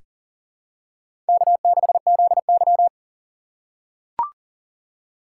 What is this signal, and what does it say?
Morse code, heard as a keyed tone, about 38 wpm, 700 Hz. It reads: K6ZY
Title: USA Call Signs - 4 Characters - 4x Character Spacing - 40wpm